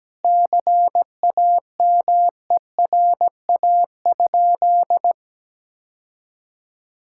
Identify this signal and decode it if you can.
Morse code, heard as a keyed tone, about 17 wpm, 700 Hz. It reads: CAMERA?
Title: Sentences from Top 1000 Words - Review Part 5 - 17wpm